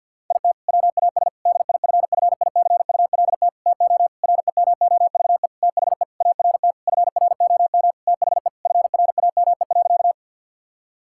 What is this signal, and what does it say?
Morse code, heard as a keyed tone, about 50 wpm, 700 Hz. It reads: IT WAS DIFFICULT TO REMOVE THE ART FROM THE FRAME.
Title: Sentences from Top 1000 Words - Review Part 6 - 50wpm